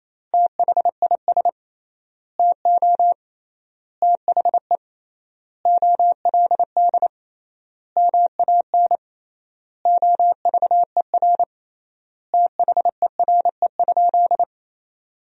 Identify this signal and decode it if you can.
Morse code, heard as a keyed tone, about 28 wpm, 700 Hz. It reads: THIS TO THE OLD MAN OVER THERE?
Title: 3x Word-Spacing - Sentences from Top 200 Words - 28wpm